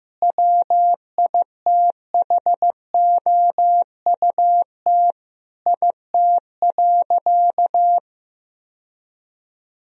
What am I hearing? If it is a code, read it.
Morse code, heard as a keyed tone, about 15 wpm, 700 Hz. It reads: WITHOUT IT.
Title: Sentences from Top 1000 Words - Review Part 3 - 15wpm